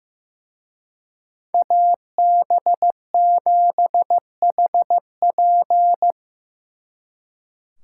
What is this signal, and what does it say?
Morse code, heard as a keyed tone, about 16 wpm, 700 Hz. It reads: AB7HP
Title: Speed Racing - USA 2x2 Call Signs 15wpm